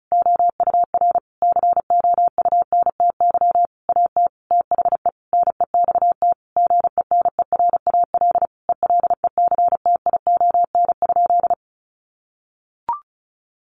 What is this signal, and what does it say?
Morse code, heard as a keyed tone, about 34 wpm, 700 Hz. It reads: OUR COUNTY AT THE NEXT GENERAL ELECTION?
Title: Sentences from Top 1500 Words 35wpm